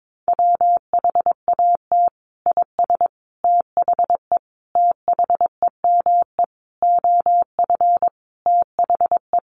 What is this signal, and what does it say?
Morse code, heard as a keyed tone, about 22 wpm, 700 Hz. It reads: WHAT IS THE THEME OF THE
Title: Sentences from Top 1100 Words 22wpm